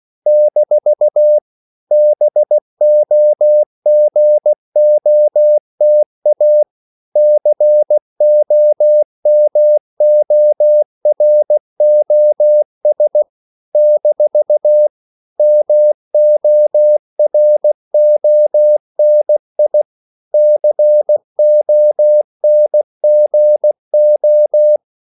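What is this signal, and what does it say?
Morse code, heard as a keyed tone, about 16 wpm, 600 Hz. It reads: - BOGOTA COMOROS - MORONI CONGO
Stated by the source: CW practice of national capitals at 16 words per minute, from ON6ZQ.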